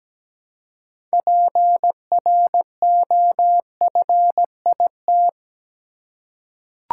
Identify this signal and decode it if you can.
Morse code, heard as a keyed tone, about 17 wpm, 700 Hz. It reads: PROFIT
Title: Six Letter Words - 17wpm